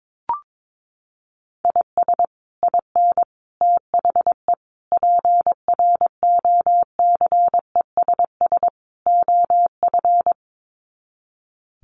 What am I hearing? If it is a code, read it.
Morse code, heard as a keyed tone, about 22 wpm, 700 Hz. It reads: IS IN THE PROCESS OF
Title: Sets of 5 Words Encore2 22wpm